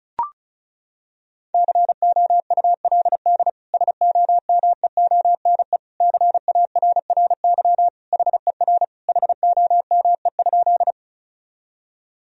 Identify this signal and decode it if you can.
Morse code, heard as a keyed tone, about 34 wpm, 700 Hz. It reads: COULD SOMEONE CARRY HER HOME?
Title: Sentences from Top 1000 Words - Review Part 4 - 35wpm